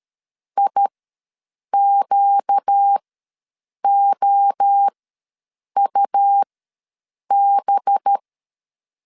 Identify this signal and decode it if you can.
Morse code, heard as a keyed tone, about 13 wpm, spaced as for 7 wpm, 800 Hz. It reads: IQOUB
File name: 5 LB ws 12wpm